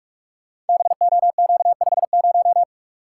Morse code, heard as a keyed tone, about 44 wpm, 700 Hz. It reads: BOQ50
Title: Code Groups - 5 Letter:Numbers - 45wpm